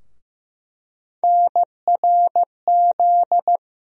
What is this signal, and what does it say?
Morse code, heard as a keyed tone, about 15 wpm, 700 Hz. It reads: NRZ
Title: USA Call Sign Postfixes - 3 Letters 15wpm